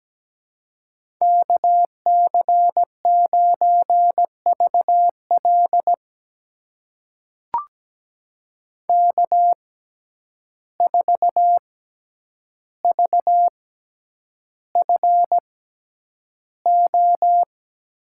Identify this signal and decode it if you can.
Morse code, heard as a keyed tone, about 17 wpm, spaced as for 9 wpm, 700 Hz. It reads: KC9VL K4VFO
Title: USA Call Signs - 5 Characters - 6x Character Spacing - 17wpm